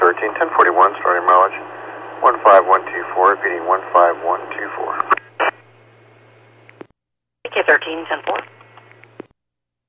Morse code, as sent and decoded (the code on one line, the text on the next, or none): none